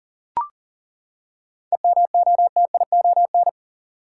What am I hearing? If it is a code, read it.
Morse code, heard as a keyed tone, about 40 wpm, 700 Hz. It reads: EMOTION
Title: Seven Letter Words - 40wpm